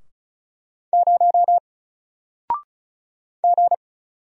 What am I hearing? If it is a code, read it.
Morse code, heard as a keyed tone, about 36 wpm, 700 Hz. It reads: 0 G
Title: Single Character (Letter:Number:Punctuation:Prosign) 35wpm